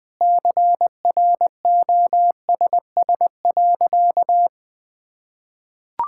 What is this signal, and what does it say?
Morse code, heard as a keyed tone, about 20 wpm, 700 Hz. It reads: CROSS.
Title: Sentences from Top 1000 Words - Review Part 6 - 20wpm